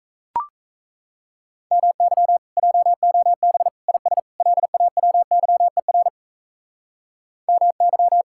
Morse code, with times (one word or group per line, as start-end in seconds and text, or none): 1.7-2.4 MY
2.6-3.7 JOB
3.9-4.2 IS
4.4-6.1 LAWYER
7.5-8.2 MY